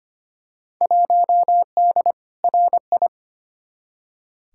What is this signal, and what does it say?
Morse code, heard as a keyed tone, about 25 wpm, 700 Hz. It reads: 1D RI
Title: Field Day Exchanges - 25wpm